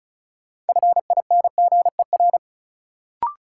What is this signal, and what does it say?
Morse code, heard as a keyed tone, about 36 wpm, 700 Hz. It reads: FINGER
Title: Sentences from Top 1600 Words - New Word Drill Alt - 35wpm